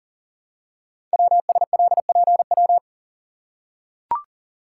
Morse code, heard as a keyed tone, about 41 wpm, 700 Hz. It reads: WSLPW
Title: Code Groups - 5 Letters - 40wpm